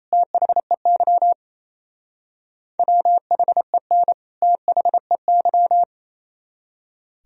Morse code, heard as a keyed tone, about 34 wpm, 700 Hz. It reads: THEY WHEN THEY
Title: Speed-Racing - Sets of 2 Words 28wpm